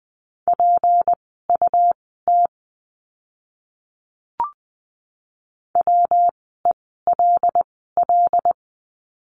Morse code, heard as a keyed tone, about 20 wpm, spaced as for 14 wpm, 700 Hz. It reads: PUT WELL
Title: 206 Lesson 33f - Letter j - Words 20wpm